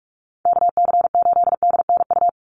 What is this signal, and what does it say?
Morse code, heard as a keyed tone, about 44 wpm, 700 Hz. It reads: KC8DNU
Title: USA Call Signs 45wpm